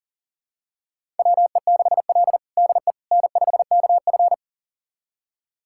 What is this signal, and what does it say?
Morse code, heard as a keyed tone, about 40 wpm, 700 Hz. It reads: WE6L DE N5KF
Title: USA Callsign DE Callsign - 4 Characters - 40wpm